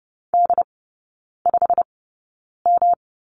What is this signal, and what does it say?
Morse code, heard as a keyed tone, about 30 wpm, spaced as for 9 wpm, 700 Hz. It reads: D5M
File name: Lesson 18e - Letter m - Three Characters 30wpm